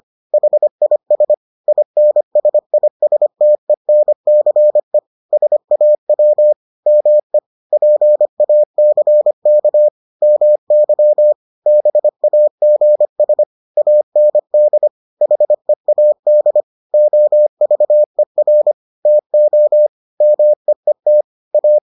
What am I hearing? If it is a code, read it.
Morse code, heard as a keyed tone, about 25 wpm, 600 Hz. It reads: HIS INSISTENCE SAW ME PACK MY BAGS AND HEAD OVER TO MEET A